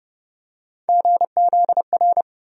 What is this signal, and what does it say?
Morse code, heard as a keyed tone, about 30 wpm, 700 Hz. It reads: GZR
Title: USA Call Sign Postfixes - 3 Letters 30wpm